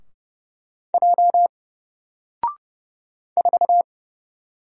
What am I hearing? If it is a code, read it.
Morse code, heard as a keyed tone, about 30 wpm, 700 Hz. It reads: J 4
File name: Single-Letter-Number-30wpm